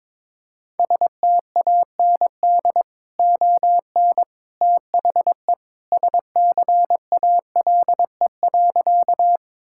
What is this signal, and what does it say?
Morse code, heard as a keyed tone, about 22 wpm, 700 Hz. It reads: STAND ON THE SCALE.
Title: Sentences from Top 1000 Words - Review Part 5 - 22wpm